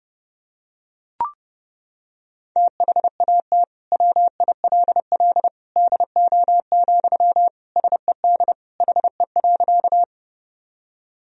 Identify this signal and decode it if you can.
Morse code, heard as a keyed tone, about 30 wpm, 700 Hz. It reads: THAT WILL DO, SED HE.
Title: Sentences from Top 100 Words and Abbreviations 30wpm